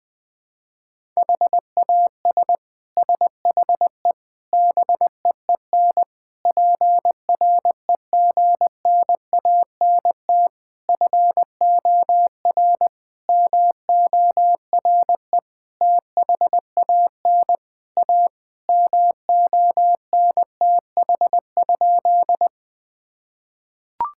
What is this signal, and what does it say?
Morse code, heard as a keyed tone, about 21 wpm, 700 Hz. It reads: HAS SHE BEEN PREGNANT FOR MORE THAN A MONTH?
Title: Sentences from Top 1000 Words - Review Part 6 - 20wpm